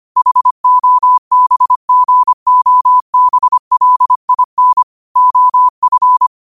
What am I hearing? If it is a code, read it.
Morse code, heard as a keyed tone, about 25 wpm, 1000 Hz. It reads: HOBGOBLIN OF